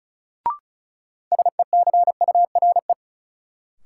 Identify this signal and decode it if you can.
Morse code, heard as a keyed tone, about 36 wpm, 700 Hz. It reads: SECURE
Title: Sentences from Top 1200 Words - New Word Drill Alt - 35wpm